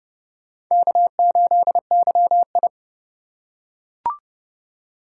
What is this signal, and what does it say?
Morse code, heard as a keyed tone, about 30 wpm, 700 Hz. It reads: K8YI
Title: USA Call Signs - 4 Characters - 3x Character Spacing - 30wpm